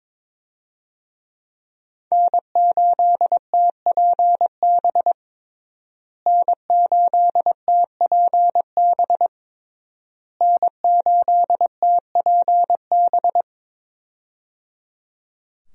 Morse code, heard as a keyed tone, about 22 wpm, 700 Hz. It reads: N8TPB N8TPB N8TPB
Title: USA Call Signs - 5 Characters - Repeated 3x - 22wpm